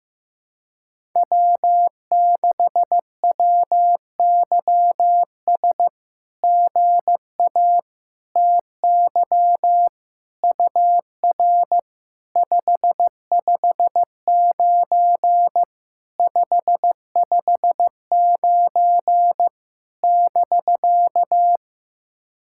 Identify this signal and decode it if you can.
Morse code, heard as a keyed tone, about 15 wpm, 700 Hz. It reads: W6WYS GA TY UR 559 559 <BK>
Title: SOTA QSO Practice 15wpm